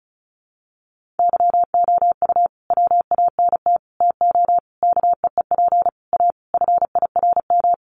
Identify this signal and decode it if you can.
Morse code, heard as a keyed tone, about 34 wpm, 700 Hz. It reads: YOU WANT TO KEEP A FIRM
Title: Sentences from Top 1400 Words 35wpm